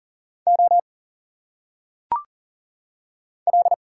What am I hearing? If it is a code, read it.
Morse code, heard as a keyed tone, about 40 wpm, 700 Hz. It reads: O L
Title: Single Letters - Phonetic Pronunciation 40wpm